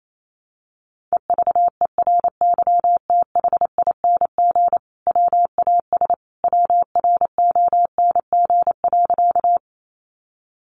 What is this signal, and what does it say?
Morse code, heard as a keyed tone, about 28 wpm, 700 Hz. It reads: EVERYTHING WAS WRONG.
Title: Sentences from Top 900 Words 28wpm